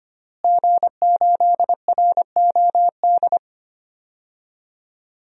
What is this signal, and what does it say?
Morse code, heard as a keyed tone, about 25 wpm, 700 Hz. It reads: G8ROD
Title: DXCC-100 Call Signs - 5 Characters - Repeated 3x - 25wpm